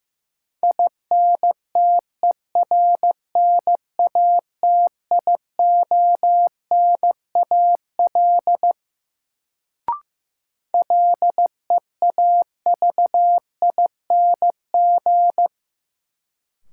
Morse code, heard as a keyed tone, about 15 wpm, 700 Hz. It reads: INTERNATIONAL LEAVING
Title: Sentences from Top 1000 Words - Review Part 4 - New Word Drill - 15wpm